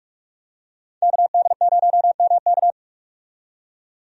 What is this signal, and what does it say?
Morse code, heard as a keyed tone, about 47 wpm, 700 Hz. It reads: KD0MK
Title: USA Call Signs - 5 Characters - 45wpm